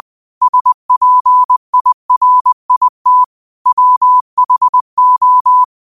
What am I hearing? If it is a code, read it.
Morse code, heard as a keyed tone, about 20 wpm, 1000 Hz. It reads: SPIRIT WHO
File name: qotdfiles / 20wpm